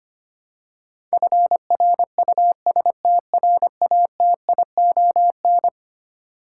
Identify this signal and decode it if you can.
Morse code, heard as a keyed tone, about 25 wpm, 700 Hz. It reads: FRUSTRATION
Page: morse-code-ninja-practice-25wpm